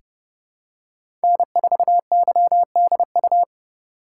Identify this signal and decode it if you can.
Morse code, heard as a keyed tone, about 30 wpm, 700 Hz. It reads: N4YDU
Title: K1USN SST QSO Practice - 30wpm